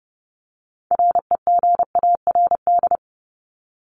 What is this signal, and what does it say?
Morse code, heard as a keyed tone, about 30 wpm, 700 Hz. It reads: REGARD
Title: ED Suffix - Top 100 Words with Base Words 30wpm